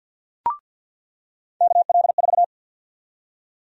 Morse code, heard as a keyed tone, about 50 wpm, 700 Hz. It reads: KL4
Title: USA Call Sign Prefixes - 50wpm